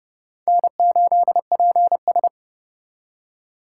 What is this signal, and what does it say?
Morse code, heard as a keyed tone, about 37 wpm, 700 Hz. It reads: N8PS
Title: Speed Racing - USA 1x2 Call Signs 30wpm